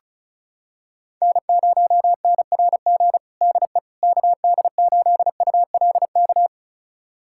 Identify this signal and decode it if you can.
Morse code, heard as a keyed tone, about 36 wpm, 700 Hz. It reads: N0NRG DE KD8ULK
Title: USA Callsign DE Callsign 35wpm